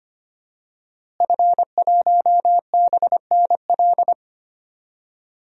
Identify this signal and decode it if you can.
Morse code, heard as a keyed tone, about 25 wpm, 700 Hz. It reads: F1BNL
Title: DXCC-100 Call Signs - 5 Characters - Repeated 2x - 25wpm